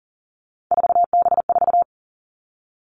Morse code, heard as a keyed tone, about 40 wpm, 700 Hz. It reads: 4B4
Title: Three Challenging Letter-Number Combinations 40wpm